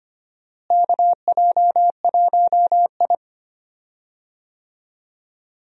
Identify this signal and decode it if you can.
Morse code, heard as a keyed tone, about 25 wpm, 700 Hz. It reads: KJ1I
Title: USA Call Signs - 4 Characters - Repeated 3x - 25wpm